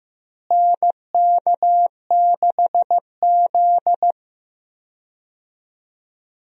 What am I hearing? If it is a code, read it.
Morse code, heard as a keyed tone, about 15 wpm, 700 Hz. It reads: NK6Z
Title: USA Call Signs - 4 Characters - Repeated 3x - 15wpm